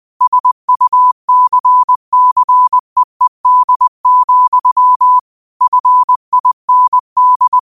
Morse code, heard as a keyed tone, about 20 wpm, 1000 Hz. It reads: SUCCEED, FIND
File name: qotdfiles / 20wpm